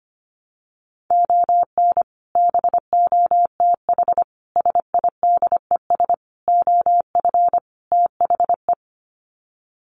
Morse code, heard as a keyed tone, about 25 wpm, 700 Hz. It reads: ON BOTH SIDES OF THE
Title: Sets of 5 Words 25wpm